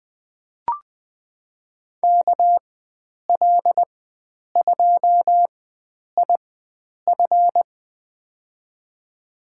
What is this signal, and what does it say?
Morse code, heard as a keyed tone, about 20 wpm, spaced as for 9 wpm, 700 Hz. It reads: KL2IF
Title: USA Call Signs - 5 Characters - 4x Character Spacing - 20wpm